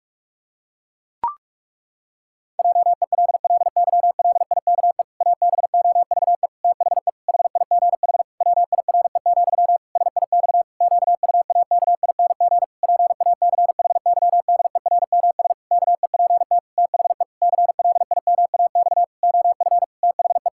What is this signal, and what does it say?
Morse code, heard as a keyed tone, about 46 wpm, 700 Hz. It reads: JELLYLIKE ABOVE THE HIGH WIRE, SIX QUAKING PACHYDERMS KEPT THE CLIMAX OF THE
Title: Pangram Challenge v2 - 45wpm